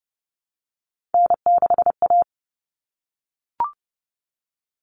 Morse code, heard as a keyed tone, about 30 wpm, 700 Hz. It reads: N6A
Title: All Cut Numbers - 3-Digit 30wpm